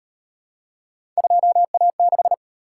Morse code, heard as a keyed tone, about 38 wpm, 700 Hz. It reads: 2A6